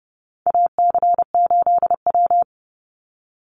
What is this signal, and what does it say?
Morse code, heard as a keyed tone, about 30 wpm, 700 Hz. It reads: AC8W
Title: Call Signs - 30wpm